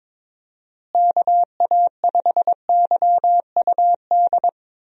Morse code, heard as a keyed tone, about 22 wpm, 700 Hz. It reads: KA5YUD
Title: USA Call Signs - 6 Characters - Repeated 2x - 22wpm